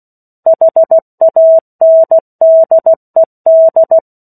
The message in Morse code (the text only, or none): HANDED